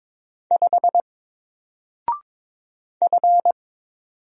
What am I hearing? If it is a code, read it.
Morse code, heard as a keyed tone, about 22 wpm, 700 Hz. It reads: 5 F
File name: Single-Letter-Number-22wpm